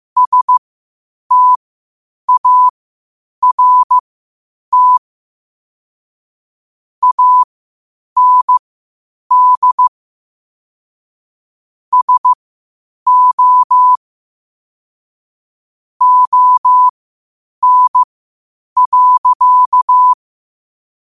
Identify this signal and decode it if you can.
Morse code, heard as a keyed tone, about 15 wpm, spaced as for 8 wpm, 1000 Hz. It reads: START AND SO ON.
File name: 8wpm